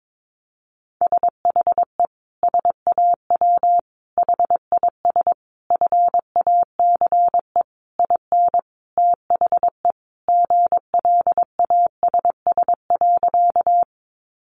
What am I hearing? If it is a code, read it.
Morse code, heard as a keyed tone, about 22 wpm, 700 Hz. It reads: SHE SAW HIS FACE IN THE GLASS.
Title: Sentences from Top 1300 Words 22wpm